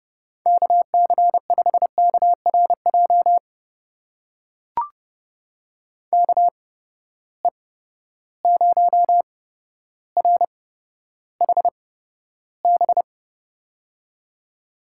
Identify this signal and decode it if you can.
Morse code, heard as a keyed tone, about 30 wpm, spaced as for 13 wpm, 700 Hz. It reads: KC5KRJ KE0RHB
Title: USA Call Signs - 6 Characters - 8x Character Spacing - 30wpm